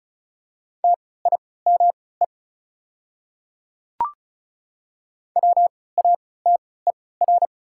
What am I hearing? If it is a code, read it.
Morse code, heard as a keyed tone, about 36 wpm, spaced as for 20 wpm, 700 Hz. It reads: TIME WATER
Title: 3x Character Spacing - Top 100 Words - 35wpm